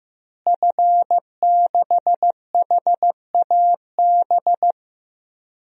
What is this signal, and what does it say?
Morse code, heard as a keyed tone, about 15 wpm, 700 Hz. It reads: F6HAB
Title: DXCC-100 Call Signs - 5 Characters - Repeated 3x - 15wpm